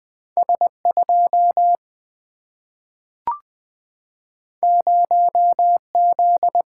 Morse code, heard as a keyed tone, about 20 wpm, 700 Hz. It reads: S2 0Z
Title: Single-Letter Single-Number 20wpm